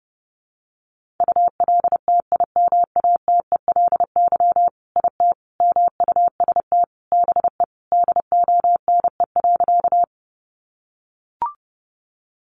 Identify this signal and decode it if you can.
Morse code, heard as a keyed tone, about 30 wpm, 700 Hz. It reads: ULTIMATELY IT MUST BE DONE.
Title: Sentences from Top 1200 Words 30wpm